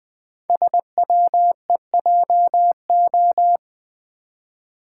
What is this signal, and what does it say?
Morse code, heard as a keyed tone, about 20 wpm, 700 Hz. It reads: SWEJO
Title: Code Groups - 5 Letters - 20wpm